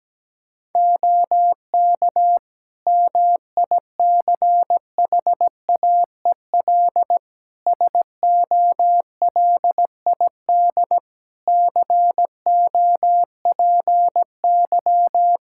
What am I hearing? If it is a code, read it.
Morse code, heard as a keyed tone, about 17 wpm, 700 Hz. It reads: OK MICHAEL SOLID COPY
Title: Practice QSO - Exchange 3 - More Info - 17wpm